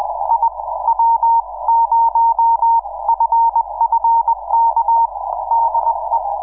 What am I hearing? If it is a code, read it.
Morse code, heard as a keyed tone, about 20 wpm, spaced as for 17 wpm, 920 Hz. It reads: IW0FFK DE
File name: iw0ffk / 2006 06 11 jj1lgo iw0ffk